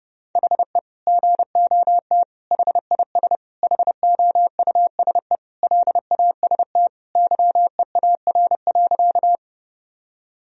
Morse code, heard as a keyed tone, about 30 wpm, 700 Hz. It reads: HE GOT HIS HOUSE LAST YEAR.
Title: Sentences from Top 1000 Words - Review Part 1 - 30wpm